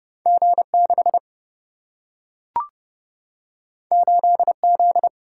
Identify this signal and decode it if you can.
Morse code, heard as a keyed tone, about 30 wpm, 700 Hz. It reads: G 6 8 Z
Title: Single-Letter Single-Number 30wpm